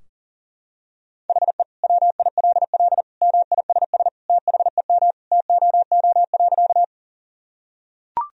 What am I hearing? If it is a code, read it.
Morse code, heard as a keyed tone, about 40 wpm, 700 Hz. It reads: HE WILL MISS THEM TOO.
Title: Sentences from Top 1000 Words - Review Part 3 - 40wpm